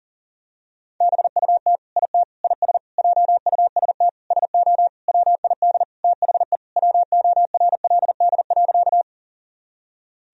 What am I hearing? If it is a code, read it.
Morse code, heard as a keyed tone, about 40 wpm, 700 Hz. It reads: BUT IT IS JUST SO WID THE WORLD.
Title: Sentences from Top 200 Words and Abbreviations 40wpm